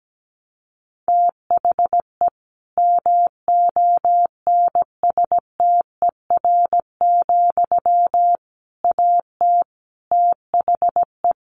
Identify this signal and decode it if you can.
Morse code, heard as a keyed tone, about 17 wpm, 700 Hz. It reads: THE MONSTER, AT THE
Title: A Sound Of Thunder 17wpm